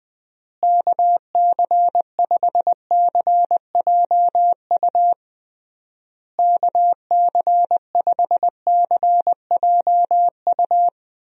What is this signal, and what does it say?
Morse code, heard as a keyed tone, about 21 wpm, 700 Hz. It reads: KC5CJU KC5CJU
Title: USA Call Signs - 6 Characters - Repeated 3x - 20wpm